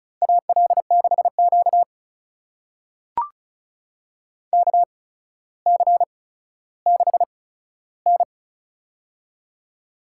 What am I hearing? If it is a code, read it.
Morse code, heard as a keyed tone, about 36 wpm, spaced as for 15 wpm, 700 Hz. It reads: AL6Q KC6N
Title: USA Call Signs - 4 Characters - 8x Character Spacing - 35wpm